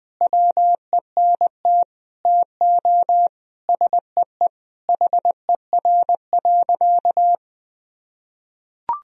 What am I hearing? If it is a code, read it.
Morse code, heard as a keyed tone, about 20 wpm, 700 Hz. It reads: WENT TO SEE HER.
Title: Sentences from Top 1000 Words - Review Part 1 - 20wpm